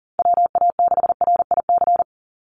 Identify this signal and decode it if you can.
Morse code, heard as a keyed tone, about 40 wpm, 700 Hz. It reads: WA6RIC
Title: NA SOTA Chaser Callsigns 40wpm